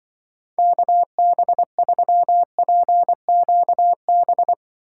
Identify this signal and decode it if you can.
Morse code, heard as a keyed tone, about 25 wpm, 700 Hz. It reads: KB3PQB
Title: Speed Racing - USA 2x3 Call Signs 20wpm